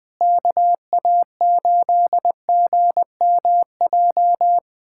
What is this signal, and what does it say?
Morse code, heard as a keyed tone, about 20 wpm, 700 Hz. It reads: KA8GMJ
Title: USA Callsign DE Callsign - 6 Characters - 20wpm